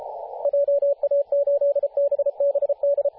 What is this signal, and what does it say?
Morse code, heard as a keyed tone, about 33 wpm, 570 Hz, with radio noise. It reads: JA8BBD
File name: posts / 2017-jan-27-0632z-a35w-14-031mhz-cw